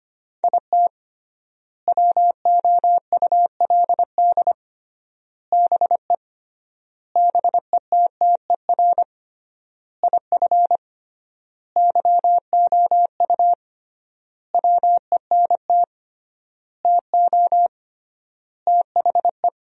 Morse code, heard as a keyed tone, about 25 wpm, 700 Hz. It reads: IT WOULD BE BETTER IF YOU WENT TO THE
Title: 3x Word-Spacing - Sentences from Top 400 Words - 25wpm